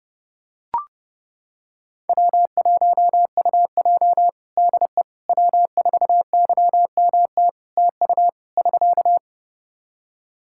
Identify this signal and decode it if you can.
Morse code, heard as a keyed tone, about 32 wpm, 700 Hz. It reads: W1UJ DE W4YMT TU <SK>
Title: Practice QSO - Exchange 4 - Ending - 30wpm